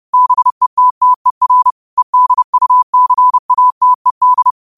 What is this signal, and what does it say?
Morse code, heard as a keyed tone, about 30 wpm, 1000 Hz. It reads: BETTER EDUCATED
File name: qotdfiles / 30wpm